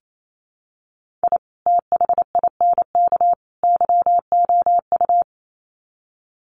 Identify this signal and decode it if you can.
Morse code, heard as a keyed tone, about 28 wpm, 700 Hz. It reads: I THINK YOU
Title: Sets of 3 Words 28wpm